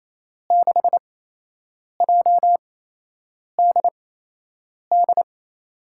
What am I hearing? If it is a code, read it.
Morse code, heard as a keyed tone, about 27 wpm, 700 Hz. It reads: 6 J D D
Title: USA Call Signs - 6 Characters - 8x Character Spacing - 28wpm